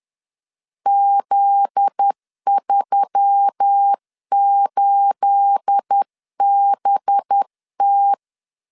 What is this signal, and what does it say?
Morse code, heard as a keyed tone, about 10 wpm, 800 Hz. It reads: Z38BT
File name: LEZIONI / 9 28 10wpm 2842